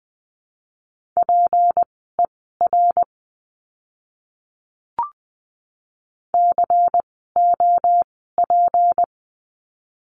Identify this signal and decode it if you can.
Morse code, heard as a keyed tone, about 20 wpm, spaced as for 14 wpm, 700 Hz. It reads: PER COP
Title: 153 Lesson 25f - Letter p - Words 20wpm